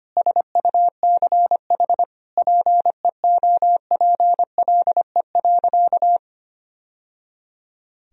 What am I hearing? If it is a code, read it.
Morse code, heard as a keyed tone, about 25 wpm, 700 Hz. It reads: SUCH PEOPLE.
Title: Sentences from Top 1000 Words - Review Part 3 - 25wpm